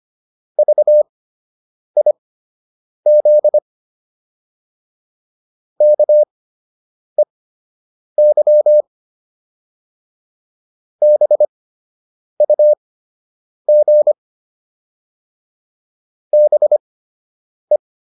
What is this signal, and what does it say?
Morse code, heard as a keyed tone, about 25 wpm, spaced as for 8 wpm, 600 Hz. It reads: VIZ KEY BUG BE